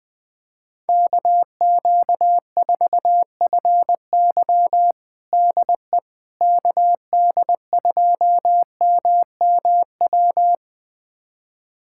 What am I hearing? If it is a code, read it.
Morse code, heard as a keyed tone, about 20 wpm, 700 Hz. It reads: KQ4FY DE KD2MMW
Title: USA Callsign DE Callsign 20wpm